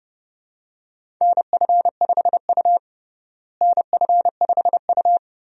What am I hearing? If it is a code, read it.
Morse code, heard as a keyed tone, about 30 wpm, 700 Hz. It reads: NF5U NF5U
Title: USA Call Signs - 4 Characters - Repeated 3x - 30wpm